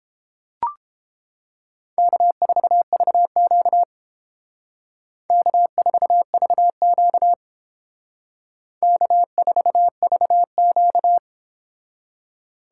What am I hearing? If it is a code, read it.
Morse code, heard as a keyed tone, about 29 wpm, 700 Hz. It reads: K4VQ K4VQ K4VQ